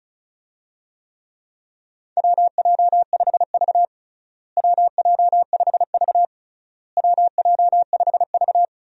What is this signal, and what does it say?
Morse code, heard as a keyed tone, about 36 wpm, 700 Hz. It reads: WJ5V WJ5V WJ5V
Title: USA Call Signs - 4 Characters - Repeated 3x - 35wpm